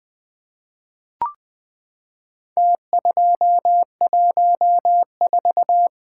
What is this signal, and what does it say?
Morse code, heard as a keyed tone, about 20 wpm, 700 Hz. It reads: T214
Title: Four Letter-Number Combinations - 20wpm